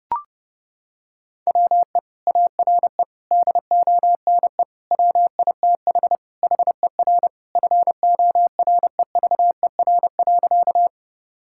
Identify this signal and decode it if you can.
Morse code, heard as a keyed tone, about 30 wpm, 700 Hz. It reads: WE ARE DONE WITH HER FOREVER.
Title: Sentences from Top 1600 Words 30wpm